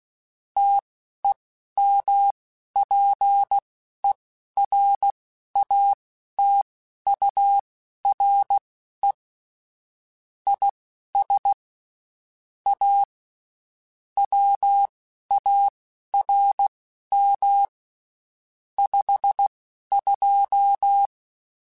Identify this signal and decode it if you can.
Morse code, heard as a keyed tone, about 16 wpm, spaced as for 11 wpm, 800 Hz. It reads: TEMPERATURE IS A WARM 52
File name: Spaced Characters / Spaced Characters-mp3s / Weather4 10 sp @15